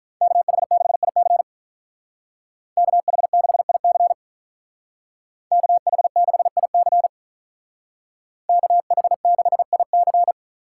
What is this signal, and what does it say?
Morse code, heard as a keyed tone, about 45 wpm, 700 Hz. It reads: KH6IC KH6IC KH6IC KH6IC